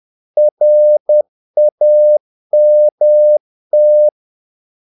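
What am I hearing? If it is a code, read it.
Morse code, heard as a keyed tone, about 10 wpm, 600 Hz. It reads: RAMT
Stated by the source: Morse practice of short Danish words at 10 wpm, from CW med Gnister.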